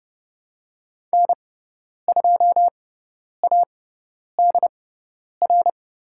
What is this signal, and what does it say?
Morse code, heard as a keyed tone, about 30 wpm, spaced as for 10 wpm, 700 Hz. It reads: N2ADR
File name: Lesson 16g - Letter c - Call Signs 30wpm